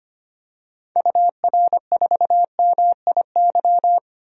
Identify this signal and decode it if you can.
Morse code, heard as a keyed tone, about 25 wpm, 700 Hz. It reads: UR4MIY
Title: DXCC-100 Call Signs - 6 Characters - 25wpm